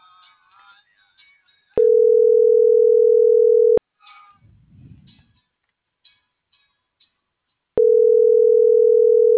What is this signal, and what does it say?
Ringback tone (North America), 440 + 480 Hz.